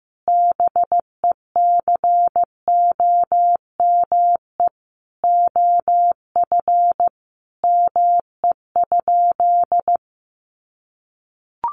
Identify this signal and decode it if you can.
Morse code, heard as a keyed tone, about 15 wpm, 700 Hz. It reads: BECOME OF ME?
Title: Sentences from Top 600 Words 15wpm